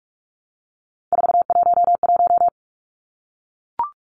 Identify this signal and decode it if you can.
Morse code, heard as a keyed tone, about 44 wpm, 700 Hz. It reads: 411
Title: Numbers - 3-Digit 45wpm